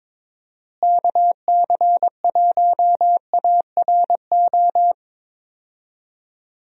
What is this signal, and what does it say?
Morse code, heard as a keyed tone, about 22 wpm, 700 Hz. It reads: KC1ARO
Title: USA Call Signs - 6 Characters - Repeated 3x - 22wpm